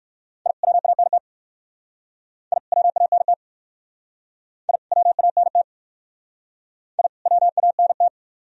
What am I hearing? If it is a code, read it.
Morse code, heard as a keyed tone, about 54 wpm, 700 Hz. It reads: I WANT I WANT I WANT I WANT
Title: Speed-Racing - Sets of 2 Words 45wpm